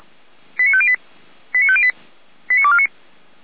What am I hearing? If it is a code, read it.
Five-tone selective call (ZVEI), heard as quick runs of five tones, tones of about 70 ms. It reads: 89508 89508 89249